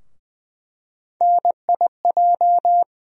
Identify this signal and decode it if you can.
Morse code, heard as a keyed tone, about 20 wpm, 700 Hz. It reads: NIJ
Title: USA Call Sign Postfixes - 3 Letters 20wpm